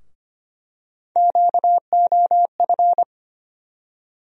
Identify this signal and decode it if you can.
Morse code, heard as a keyed tone, about 25 wpm, 700 Hz. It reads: QOF
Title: USA Call Sign Postfixes - 3 Letters 25wpm